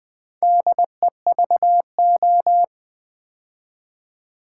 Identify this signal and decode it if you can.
Morse code, heard as a keyed tone, about 20 wpm, 700 Hz. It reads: DEVO
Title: Build A Word - 2 Letters At A Time - 6-Letter Words - 20wpm